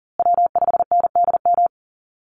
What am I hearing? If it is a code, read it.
Morse code, heard as a keyed tone, about 39 wpm, 700 Hz. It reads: W5NDM
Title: USA Call Signs Repeated 2x 40wpm